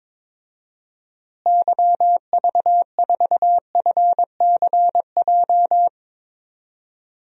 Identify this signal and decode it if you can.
Morse code, heard as a keyed tone, about 22 wpm, 700 Hz. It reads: YV4FCJ
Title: DXCC-100 Call Signs - 6 Characters - Repeated 3x - 22wpm